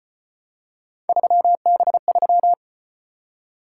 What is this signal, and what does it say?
Morse code, heard as a keyed tone, about 34 wpm, 700 Hz. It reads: 3B3